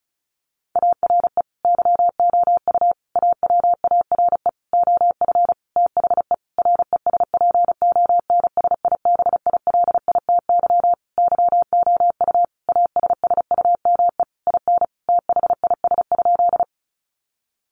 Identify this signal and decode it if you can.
Morse code, heard as a keyed tone, about 34 wpm, 700 Hz. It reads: ARE YOU AWARE OF THE RESPONSIBILITY YOU ASSUME IN THIS?
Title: Sentences from Top 1500 Words 35wpm